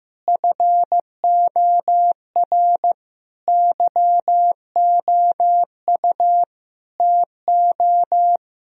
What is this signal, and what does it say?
Morse code, heard as a keyed tone, about 15 wpm, 700 Hz. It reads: FOR YOU TO
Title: Sentences from Top 1000 Words - Review Part 3 - 15wpm